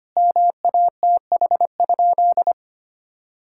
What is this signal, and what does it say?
Morse code, heard as a keyed tone, about 25 wpm, 700 Hz. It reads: MATH?
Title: Sentences from Top 1000 Words - Review Part 6 - 25wpm